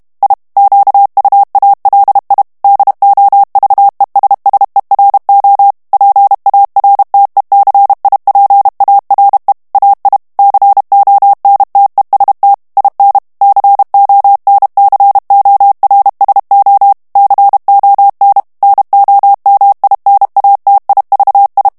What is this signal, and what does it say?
Morse code, heard as a keyed tone, about 32 wpm, 800 Hz. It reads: I QUALI DOVESSERO PARTECIPARE AI CONTEST IN CONCORSO CON NOMINATIVI
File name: ITALIEN texte clair 2 vitesse 30 mots minute